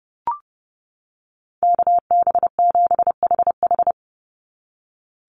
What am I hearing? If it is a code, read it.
Morse code, heard as a keyed tone, about 30 wpm, 700 Hz. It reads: KB7HH
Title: NA SOTA Chaser Callsigns 30wpm